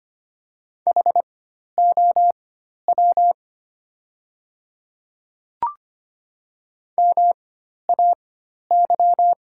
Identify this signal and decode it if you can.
Morse code, heard as a keyed tone, about 25 wpm, spaced as for 12 wpm, 690 Hz. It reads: HOW MAY
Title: 4x Character Spacing - Top 100 Words - 25wpm